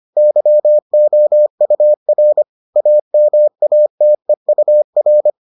Morse code, heard as a keyed tone, about 25 wpm, 600 Hz. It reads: YOUR AMATEUR